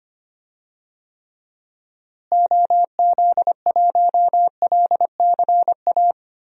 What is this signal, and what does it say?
Morse code, heard as a keyed tone, about 25 wpm, 700 Hz. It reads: OZ1LCA
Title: DXCC-100 Call Signs - 6 Characters - Repeated 2x - 25wpm